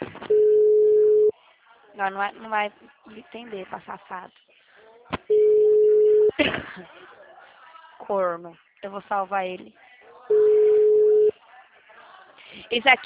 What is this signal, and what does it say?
Ringback tone (Europe, 425 Hz), with speech.
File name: out- -2011- - - .6993